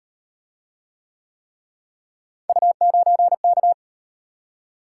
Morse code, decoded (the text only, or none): U9K